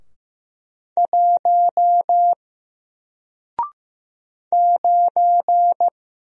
Morse code, heard as a keyed tone, about 15 wpm, 700 Hz. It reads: 1 9
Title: Single Letter-Number Alt 15wpm